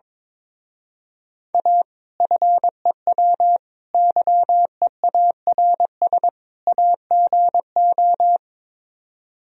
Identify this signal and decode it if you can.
Morse code, heard as a keyed tone, about 22 wpm, 700 Hz. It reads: A FEW YEARS AGO
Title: Speed-Racing - Sets of 4 Words 22wpm